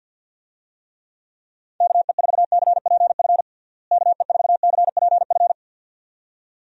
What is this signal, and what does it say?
Morse code, heard as a keyed tone, about 51 wpm, 700 Hz. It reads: KE4CPF KE4CPF
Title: USA Call Signs - 6 Characters - Repeated 2x - 50wpm